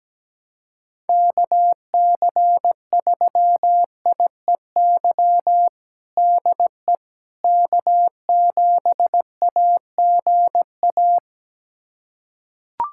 Morse code, heard as a keyed tone, about 17 wpm, 700 Hz. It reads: KC3IEY DE K7AGA
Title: USA Callsign DE Callsign 17wpm